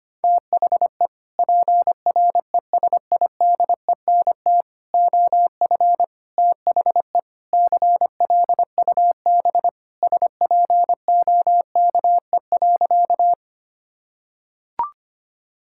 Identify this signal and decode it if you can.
Morse code, heard as a keyed tone, about 25 wpm, 700 Hz. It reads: THE PRESIDENT OF THE CLUB SPOKE.
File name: practice / 25wpm / Sentences with 4-Word Phrases - 25wpm